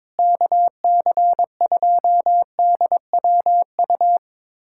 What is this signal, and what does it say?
Morse code, heard as a keyed tone, about 22 wpm, 700 Hz. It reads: KC2DWU
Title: USA Callsign DE Callsign - 6 Characters - 22wpm